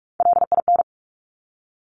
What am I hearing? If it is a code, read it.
Morse code, heard as a keyed tone, about 46 wpm, 700 Hz. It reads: LIN
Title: Top Three Letter Combinations in Words 45wpm